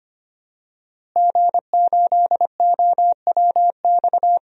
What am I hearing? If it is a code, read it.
Morse code, heard as a keyed tone, about 25 wpm, 700 Hz. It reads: G8OWX
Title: DXCC-100 Call Signs - 5 Characters - Repeated 3x - 25wpm